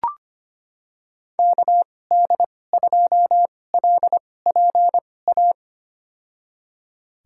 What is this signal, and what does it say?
Morse code, heard as a keyed tone, about 25 wpm, spaced as for 18 wpm, 700 Hz. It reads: KD2LPA
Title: USA Call Signs - 6 Characters - 2x Character Spacing - 25wpm